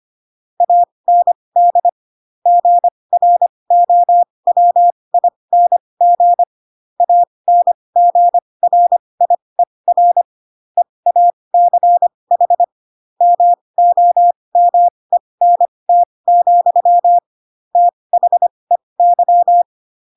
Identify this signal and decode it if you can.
Morse code, heard as a keyed tone, about 25 wpm, spaced as for 20 wpm, 700 Hz. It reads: AND GROWING ANGRIER EACH MOMENT, THEY
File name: Pinocchio / Pinocchio ch2 15 25 ik1hge